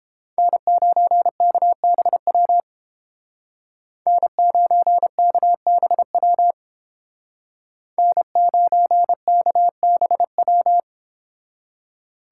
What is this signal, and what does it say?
Morse code, heard as a keyed tone, about 28 wpm, 700 Hz. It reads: N9KBW N9KBW N9KBW